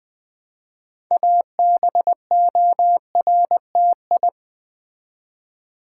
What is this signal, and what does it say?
Morse code, heard as a keyed tone, about 20 wpm, 700 Hz. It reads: ABORTI
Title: Build A Word - 3 Letters At A Time - 8-Letter Words - 20wpm